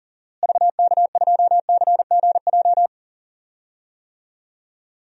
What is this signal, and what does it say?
Morse code, heard as a keyed tone, about 40 wpm, 700 Hz. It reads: VK2CGJ
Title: DXCC-100 Call Signs - 6 Characters - Repeated 3x - 40wpm